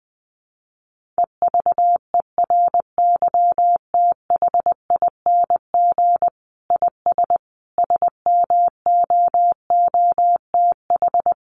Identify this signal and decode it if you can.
Morse code, heard as a keyed tone, about 20 wpm, 700 Hz. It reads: EVERYTHING IS SMOOTH
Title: Sentences from Top 1400 Words 20wpm